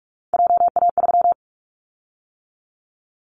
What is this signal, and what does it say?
Morse code, heard as a keyed tone, about 46 wpm, 700 Hz. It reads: JA3
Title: Callsign Prefixes for Top 50 DXCC - 45wpm